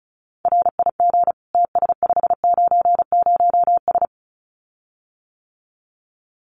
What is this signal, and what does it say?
Morse code, heard as a keyed tone, about 34 wpm, 700 Hz. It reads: RIG TS590S
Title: Common QSO Phrases Encore 35wpm